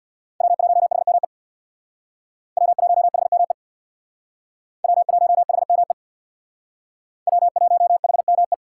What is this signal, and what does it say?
Morse code, heard as a keyed tone, about 65 wpm, 700 Hz. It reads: W1HME W1HME W1HME W1HME